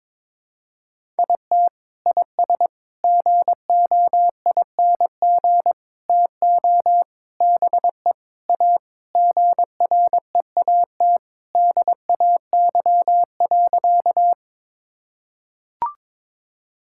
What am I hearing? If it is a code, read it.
Morse code, heard as a keyed tone, about 21 wpm, 700 Hz. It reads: IT IS GOING TO BE A GREAT DAY.
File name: practice / 22wpm / Sentences with 3-Word Phrases - 22wpm